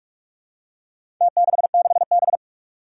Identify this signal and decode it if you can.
Morse code, heard as a keyed tone, about 46 wpm, 700 Hz. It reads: T66B
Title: Four Challenging Letter-Number Combinations 45wpm